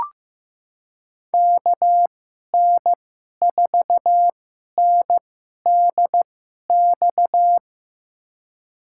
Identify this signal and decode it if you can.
Morse code, heard as a keyed tone, about 15 wpm, spaced as for 11 wpm, 700 Hz. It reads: KN4NDX
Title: USA Call Signs - 6 Characters - 2x Character Spacing - 15wpm